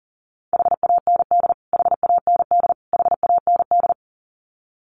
Morse code, heard as a keyed tone, about 40 wpm, 700 Hz. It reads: HAND HAND HAND
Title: Top 200 Words Repeated 40wpm